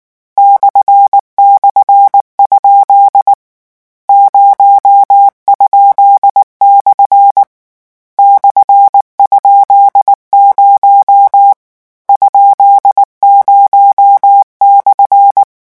Morse code, heard as a keyed tone, about 19 wpm, 800 Hz. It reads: //? 0?/ /?0 ?0/
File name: morse mp3 / lecon13-vitesse 18